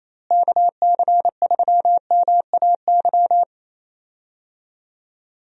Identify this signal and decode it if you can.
Morse code, heard as a keyed tone, about 28 wpm, 700 Hz. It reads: KC3MAY
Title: USA Call Signs - 6 Characters - Repeated 3x - 28wpm